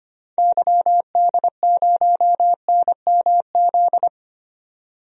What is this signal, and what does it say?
Morse code, heard as a keyed tone, about 25 wpm, 690 Hz. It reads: YD0NMZ
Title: DXCC-100 Call Signs - 6 Characters - 25wpm